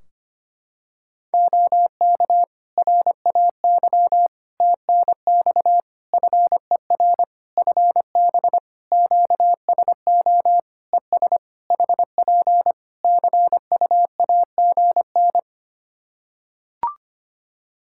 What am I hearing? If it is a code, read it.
Morse code, heard as a keyed tone, about 25 wpm, 700 Hz. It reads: OK RAY TNX FER FB QSO ES HP CUAGN
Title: Practice QSO - Exchange 4 - Ending - 25wpm